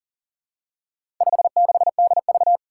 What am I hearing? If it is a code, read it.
Morse code, heard as a keyed tone, about 40 wpm, 700 Hz. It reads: 56DV
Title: Four Challenging Letter-Number Combinations 40wpm